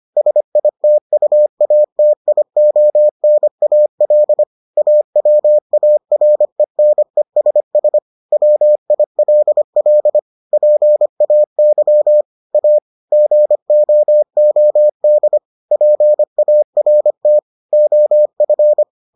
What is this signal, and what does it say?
Morse code, heard as a keyed tone, about 25 wpm, 600 Hz. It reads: SITUATIONAL AWARENESS WILL PAY A GOOD PART OF